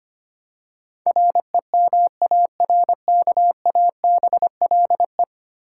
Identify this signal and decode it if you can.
Morse code, heard as a keyed tone, about 25 wpm, 700 Hz. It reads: REMARKABLE
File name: 10 Letter Words - First 3 Letters Given - 25wpm